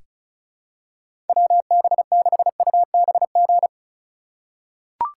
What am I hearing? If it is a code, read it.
Morse code, heard as a keyed tone, about 34 wpm, 700 Hz. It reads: WB6UBG
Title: USA Call Signs - 6 Characters - 4x Character Spacing - 35wpm